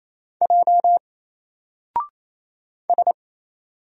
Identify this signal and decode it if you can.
Morse code, heard as a keyed tone, about 27 wpm, 700 Hz. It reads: J S
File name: Single-Letter-Number-28wpm